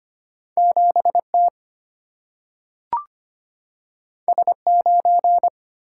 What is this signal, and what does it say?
Morse code, heard as a keyed tone, about 25 wpm, 700 Hz. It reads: 7T S9
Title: Single-Letter Single-Number 25wpm